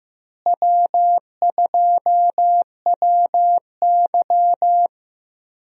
Morse code, heard as a keyed tone, about 15 wpm, 700 Hz. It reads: W2WY
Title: USA Call Signs - 4 Characters - Repeated 3x - 15wpm